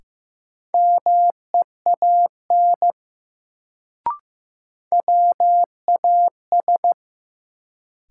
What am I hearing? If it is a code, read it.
Morse code, heard as a keyed tone, about 15 wpm, 700 Hz. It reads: MEAN WAS
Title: Sets of 2 Words - New Word Drill - 15wpm